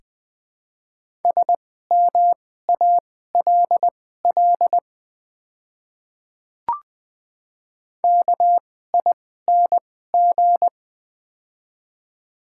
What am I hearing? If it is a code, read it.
Morse code, heard as a keyed tone, about 20 wpm, spaced as for 14 wpm, 700 Hz. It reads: SMALL KING
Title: 2x Character Spacing - Top 500 Words - 20wpm